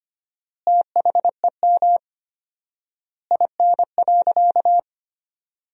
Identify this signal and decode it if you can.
Morse code, heard as a keyed tone, about 25 wpm, 700 Hz. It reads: THEM IN.
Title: 4x Word-Spacing - Sentences from Top 500 Words - 25wpm